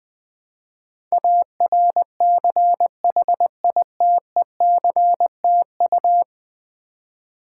Morse code, heard as a keyed tone, about 20 wpm, 700 Hz. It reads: ARCHITECTU
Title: Build A Word - 5 Letters At A Time - 12-Letter Words - 20wpm